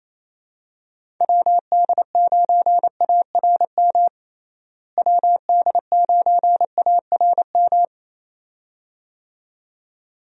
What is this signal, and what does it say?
Morse code, heard as a keyed tone, about 28 wpm, 700 Hz. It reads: WD9ARM WD9ARM
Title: USA Call Signs - 6 Characters - Repeated 2x - 28wpm